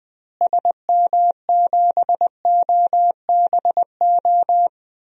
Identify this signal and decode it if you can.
Morse code, heard as a keyed tone, about 20 wpm, 700 Hz. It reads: SM7OBO
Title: DXCC-100 Call Signs - 6 Characters - Repeated 3x - 20wpm